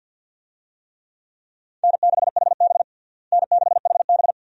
Morse code, heard as a keyed tone, about 53 wpm, 700 Hz. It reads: N6HB N6HB
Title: USA Call Signs - 4 Characters - Repeated 2x - 50wpm